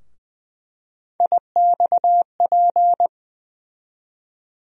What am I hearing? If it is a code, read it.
Morse code, heard as a keyed tone, about 20 wpm, 700 Hz. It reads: IXP
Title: USA Call Sign Postfixes - 3 Letters 20wpm